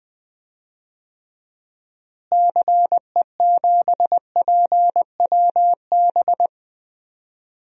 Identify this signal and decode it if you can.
Morse code, heard as a keyed tone, about 20 wpm, 700 Hz. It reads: CE7PWB
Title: DXCC-100 Call Signs - 6 Characters - Repeated 2x - 20wpm